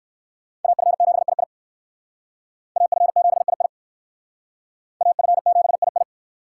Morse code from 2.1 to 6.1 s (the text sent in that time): AF7II AF7II